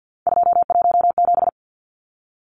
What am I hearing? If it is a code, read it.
Morse code, heard as a keyed tone, about 50 wpm, 700 Hz. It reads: 217
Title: Numbers - 3-Digit 50wpm